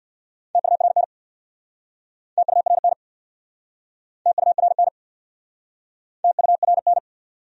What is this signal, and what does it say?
Morse code, heard as a keyed tone, about 67 wpm, 700 Hz. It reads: TURN TURN TURN TURN